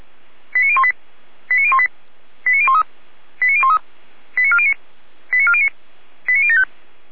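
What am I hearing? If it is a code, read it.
Five-tone selective call (ZVEI), heard as quick runs of five tones, tones of about 70 ms. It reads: 89018 89018 89013 89013 89409 89409 89075
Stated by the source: ZVEI five-tone selective call 89560 received by SDR.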